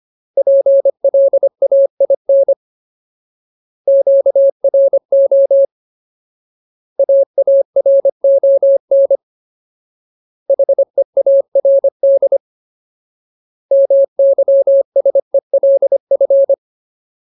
Morse code, heard as a keyed tone, about 25 wpm, 550 Hz. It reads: PLAIN QRO AARON HEARD MYSELF
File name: Ultimate Word List / 25WPM Ultimate